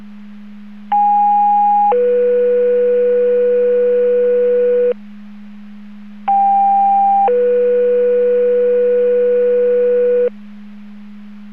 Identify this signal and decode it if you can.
Two-tone page, heard as a one-second tone, then a three-second tone, about 820 then 490 Hz, about 800 then 490 Hz. It is heard from 0.9 to 10.3 s.